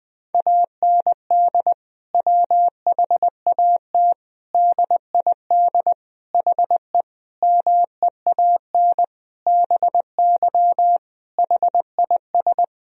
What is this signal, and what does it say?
Morse code, heard as a keyed tone, about 21 wpm, 700 Hz. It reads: AND WHAT DID HE MEAN BY HIS
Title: Sentences from Top 1000 Words - Review Part 3 - 20wpm